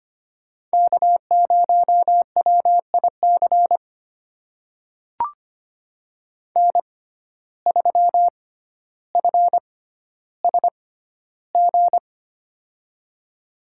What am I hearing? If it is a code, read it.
Morse code, heard as a keyed tone, about 25 wpm, spaced as for 13 wpm, 700 Hz. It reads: K0WIC N3FSG
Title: USA Call Signs - 5 Characters - 6x Character Spacing - 25wpm